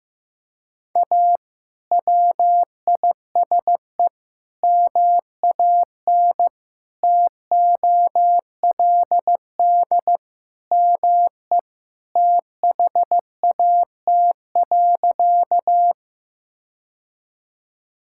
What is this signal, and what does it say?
Morse code, heard as a keyed tone, about 15 wpm, 700 Hz. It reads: A WISE MAN TOLD ME THAT.
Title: Sentences from Top 1600 Words 15wpm